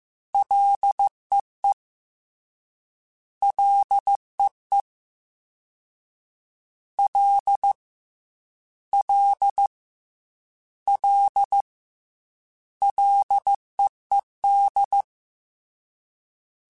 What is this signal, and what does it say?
Morse code, heard as a keyed tone, about 15 wpm, 800 Hz. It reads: LEE LEE L L L LEED
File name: F and L / F and L mp3 / F and L Prefixes 3x2 Twice @13